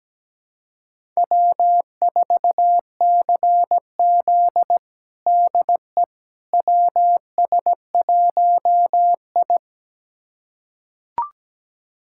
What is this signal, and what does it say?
Morse code, heard as a keyed tone, about 17 wpm, 700 Hz. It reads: W4CZ DE WS1I
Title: USA Callsign DE Callsign - 4 Characters - 17wpm